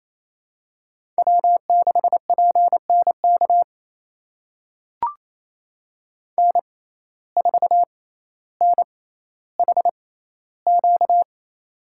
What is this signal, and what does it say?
Morse code, heard as a keyed tone, about 27 wpm, spaced as for 14 wpm, 700 Hz. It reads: W6PNK N4NHQ
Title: USA Call Signs - 5 Characters - 6x Character Spacing - 28wpm